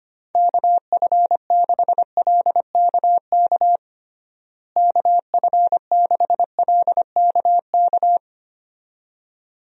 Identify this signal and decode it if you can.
Morse code, heard as a keyed tone, about 25 wpm, 700 Hz. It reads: KF6LKK KF6LKK
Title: USA Call Signs - 6 Characters - Repeated 2x - 25wpm